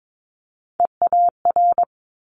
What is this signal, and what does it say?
Morse code, heard as a keyed tone, about 22 wpm, 700 Hz. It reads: EAR
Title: Multi-letter Phonemes Quick Fire 22wpm